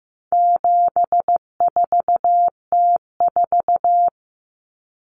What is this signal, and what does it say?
Morse code, heard as a keyed tone, about 15 wpm, 700 Hz. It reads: 74T4
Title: Common Cut Numbers - 4-Digit 15wpm